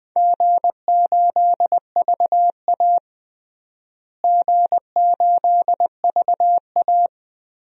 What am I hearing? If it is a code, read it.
Morse code, heard as a keyed tone, about 19 wpm, 700 Hz. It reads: G8VA G8VA
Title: DXCC-100 Call Signs - 4 Characters - Repeated 3x - 20wpm